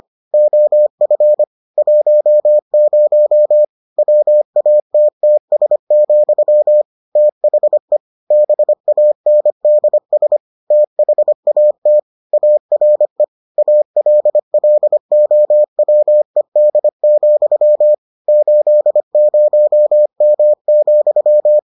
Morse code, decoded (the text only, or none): OF 10 WATTS, THE BANDS THAT ARE ALLOWED, 80M,